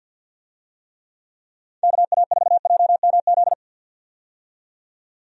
Morse code, heard as a keyed tone, about 50 wpm, 700 Hz. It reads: KA4JMZ
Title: USA Call Signs - 6 Characters - 50wpm